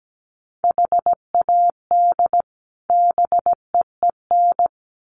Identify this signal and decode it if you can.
Morse code, heard as a keyed tone, about 17 wpm, 700 Hz. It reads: HAD BEEN
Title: Sets of 2 Words 17wpm